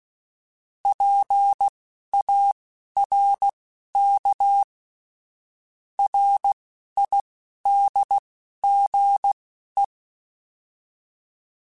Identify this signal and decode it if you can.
Morse code, heard as a keyed tone, about 16 wpm, spaced as for 11 wpm, 800 Hz. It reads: PARK RIDGE
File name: fromSMs / FromSMs mp3s / Cities@10